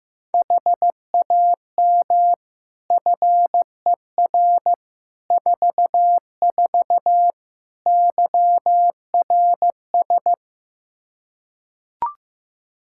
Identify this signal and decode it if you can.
Morse code, heard as a keyed tone, about 16 wpm, 700 Hz. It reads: HAM FER 44 YRS
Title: Practice QSO - Exchange 3 - More Info - 15wpm